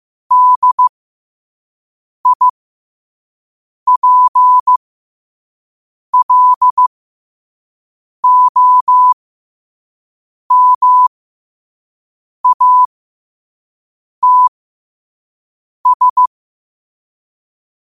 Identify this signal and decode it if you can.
Morse code, heard as a keyed tone, about 15 wpm, spaced as for 5 wpm, 1000 Hz. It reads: DIPLOMATS
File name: qotdfiles / 5wpm